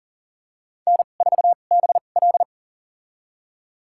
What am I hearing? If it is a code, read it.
Morse code, heard as a keyed tone, about 37 wpm, spaced as for 28 wpm, 700 Hz. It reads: N4BL
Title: USA Call Signs - 4 Characters - 2x Character Spacing - 40wpm